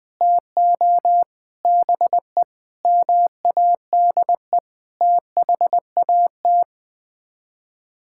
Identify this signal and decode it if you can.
Morse code, heard as a keyed tone, about 20 wpm, 700 Hz. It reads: TO BE MADE THAT
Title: Sets of 8 Words - 20wpm